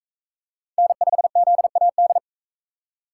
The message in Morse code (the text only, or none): N5ZAD